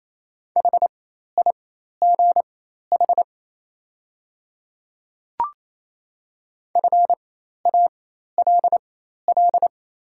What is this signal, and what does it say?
Morse code, heard as a keyed tone, about 28 wpm, spaced as for 13 wpm, 700 Hz. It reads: HIGH FALL
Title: 4x Character Spacing - Top 500 Words - 28wpm